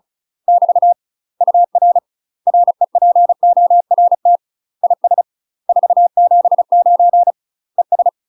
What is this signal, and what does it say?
Morse code, heard as a keyed tone, about 36 wpm, 700 Hz. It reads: <BT> UR REPORT IS 479 ES
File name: UK MASTER QSOs TRAINING / 35WPM / UK MASTER QSOs 0020